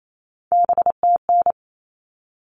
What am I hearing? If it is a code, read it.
Morse code, heard as a keyed tone, about 28 wpm, 700 Hz. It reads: BTN
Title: All Cut Numbers - 3-Digit 28wpm